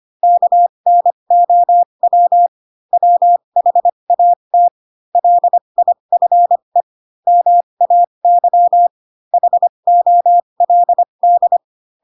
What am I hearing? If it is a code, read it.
Morse code, heard as a keyed tone, about 25 wpm, spaced as for 22 wpm, 700 Hz. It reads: KNOW WHAT LIFE MAY HOLD
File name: Pinocchio / Pinocchio ch7 18 25 ik1hge